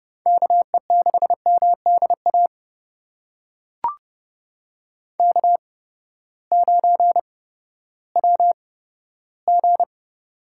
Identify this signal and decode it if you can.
Morse code, heard as a keyed tone, about 30 wpm, spaced as for 15 wpm, 700 Hz. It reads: KE6MDA K9WG
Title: USA Call Signs - 8x Character Spacing - 30wpm